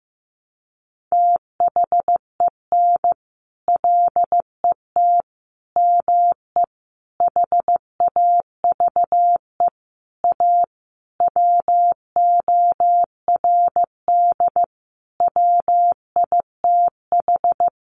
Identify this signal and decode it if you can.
Morse code, heard as a keyed tone, about 15 wpm, 700 Hz. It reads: THEN LET ME HAVE A WORD WITH
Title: Sentences from Top 300 Words 15wpm